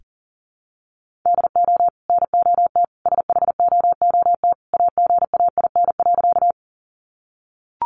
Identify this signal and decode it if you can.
Morse code, heard as a keyed tone, about 40 wpm, 700 Hz. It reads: DO NOT SHOOT AGAIN.
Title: Sentences from Top 1200 Words 40wpm